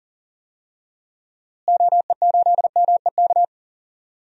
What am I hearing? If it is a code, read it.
Morse code, heard as a keyed tone, about 40 wpm, 700 Hz. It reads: OE8MEK
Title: DXCC-100 Call Signs - 6 Characters - Repeated 2x - 40wpm